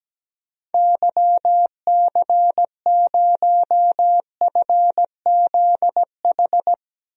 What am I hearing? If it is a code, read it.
Morse code, heard as a keyed tone, about 17 wpm, 700 Hz. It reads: YC0FZH
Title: DXCC-100 Call Signs - 6 Characters - Repeated 2x - 17wpm